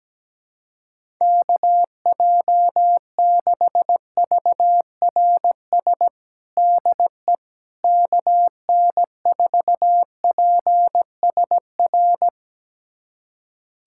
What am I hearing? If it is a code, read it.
Morse code, heard as a keyed tone, about 17 wpm, 700 Hz. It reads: KJ6VRS DE KN4PSR
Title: USA Callsign DE Callsign - 6 Characters - 17wpm